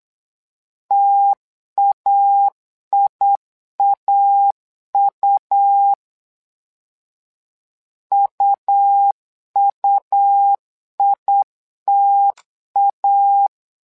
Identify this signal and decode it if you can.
Morse code, heard as a keyed tone, about 8 wpm, 800 Hz. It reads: TAIAU UUITA
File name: LEZIONI / 2 UHDN / 2 U 8wpm 2028